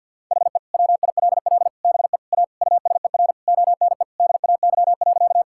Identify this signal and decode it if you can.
Morse code, heard as a keyed tone, about 50 wpm, 700 Hz. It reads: HE WILL BE A USER ONE DAY.
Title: Sentences from Top 1000 Words - Review Part 5 - 50wpm